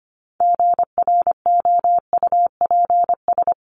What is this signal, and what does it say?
Morse code, heard as a keyed tone, about 25 wpm, 700 Hz. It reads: GROUPS
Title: ES:S Suffix - Top 100 Words with Base Words 25wpm